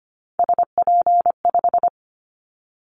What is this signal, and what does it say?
Morse code, heard as a keyed tone, about 25 wpm, 700 Hz. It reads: SP5
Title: Callsign Prefixes for Top 25 DXCC - 25wpm